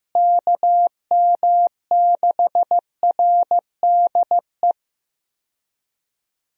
Morse code, heard as a keyed tone, about 15 wpm, 700 Hz. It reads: KM6RDE
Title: USA Call Signs - 6 Characters - Repeated 2x - 15wpm